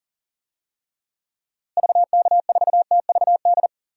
Morse code, heard as a keyed tone, about 37 wpm, 700 Hz. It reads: VK4TVD
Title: DXCC-100 Call Signs - 6 Characters - Repeated 2x - 40wpm